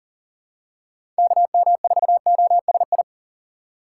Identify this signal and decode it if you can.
Morse code, heard as a keyed tone, about 40 wpm, 700 Hz. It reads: KM4OSI
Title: USA Call Signs - 6 Characters - 6x Character Spacing - 40wpm